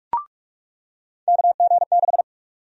Morse code, heard as a keyed tone, about 46 wpm, 700 Hz. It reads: KG6
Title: USA Call Sign Prefixes - 45wpm